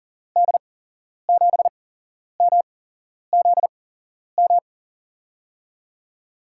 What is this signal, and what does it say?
Morse code, heard as a keyed tone, about 39 wpm, spaced as for 11 wpm, 700 Hz. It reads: D7MZM
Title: USA Call Signs - 6 Characters - 8x Character Spacing - 40wpm